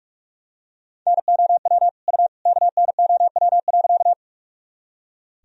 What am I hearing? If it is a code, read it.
Morse code, heard as a keyed tone, about 46 wpm, 700 Hz. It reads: NOW U KNOW.
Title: Sentences from Top 100 Words and Abbreviations 45wpm